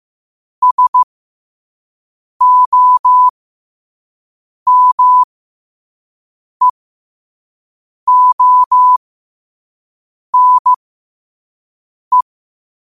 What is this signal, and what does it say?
Morse code, heard as a keyed tone, about 15 wpm, spaced as for 5 wpm, 1000 Hz. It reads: SOMEONE
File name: qotdfiles / 5wpm